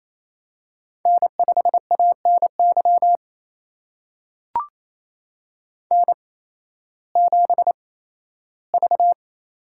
Morse code, heard as a keyed tone, about 28 wpm, spaced as for 15 wpm, 700 Hz. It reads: N5ANY N7V
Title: USA Call Signs - 5 Characters - 8x Character Spacing - 28wpm